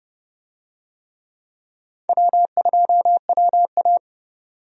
Morse code, heard as a keyed tone, about 31 wpm, 700 Hz. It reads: W2WA
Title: USA Call Signs - 4 Characters - 30wpm